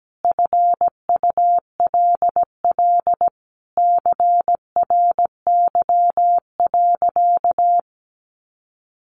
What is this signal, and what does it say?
Morse code, heard as a keyed tone, about 17 wpm, 700 Hz. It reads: FULL CRY.
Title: Sentences from Top 1500 Words 17wpm